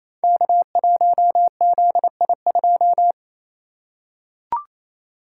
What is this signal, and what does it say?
Morse code, heard as a keyed tone, about 27 wpm, 700 Hz. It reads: K1ZI2
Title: Code Groups - 5 Letter:Numbers - 28wpm